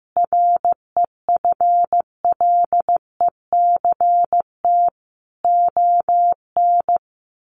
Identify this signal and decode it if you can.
Morse code, heard as a keyed tone, about 15 wpm, 700 Hz. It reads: REFLECT ON
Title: Sentences from Top 1500 Words 15wpm